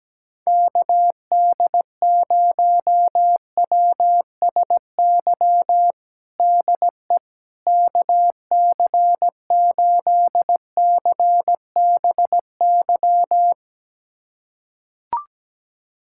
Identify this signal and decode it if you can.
Morse code, heard as a keyed tone, about 17 wpm, 700 Hz. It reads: KD0WSY DE KC8CBY
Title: USA Callsign DE Callsign 17wpm